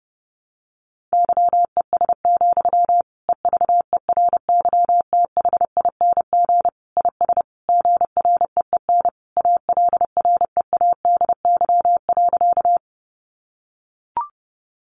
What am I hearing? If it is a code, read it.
Morse code, heard as a keyed tone, about 30 wpm, 700 Hz. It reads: YES, EVERYTHING IS GREEN ALREADY.
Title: Sentences from Top 700 Words 30wpm